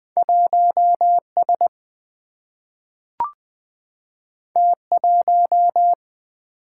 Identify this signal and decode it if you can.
Morse code, heard as a keyed tone, about 20 wpm, 700 Hz. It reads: 1S T1
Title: Single-Letter Single-Number 20wpm